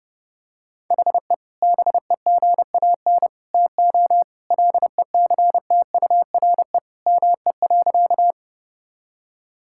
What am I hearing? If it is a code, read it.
Morse code, heard as a keyed tone, about 32 wpm, 700 Hz. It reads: HE BEGAN TO LECTURE ME.
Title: Sentences from Top 1600 Words 30wpm